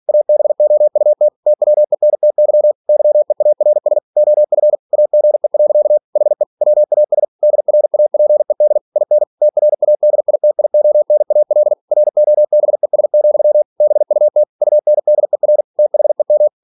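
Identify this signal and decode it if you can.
Morse code, heard as a keyed tone, about 46 wpm, 600 Hz. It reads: ABOUT TWENTY YEARS OF AGE. HE WAS DRAPED IN TRADITIONAL ROBES, BUT UNDER THEM
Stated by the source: Practice text from an English novel sent at 47 WPM.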